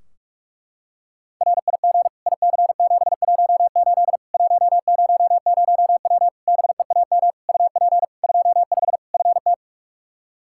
Morse code, heard as a keyed tone, about 46 wpm, 700 Hz. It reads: RIG IC718 100W BEAM UP 25 FT
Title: Practice QSO - Exchange 2 - Info - 45wpm